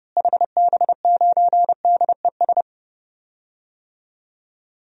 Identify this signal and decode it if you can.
Morse code, heard as a keyed tone, about 30 wpm, 700 Hz. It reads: HB9DES
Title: DXCC-100 Call Signs - 6 Characters - Repeated 2x - 30wpm